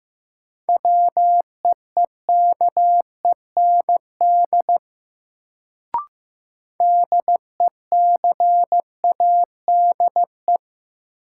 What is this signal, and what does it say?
Morse code, heard as a keyed tone, about 16 wpm, 700 Hz. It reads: WEEKEND DECADE
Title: Sets of 5 Words - New Word Drill - 15wpm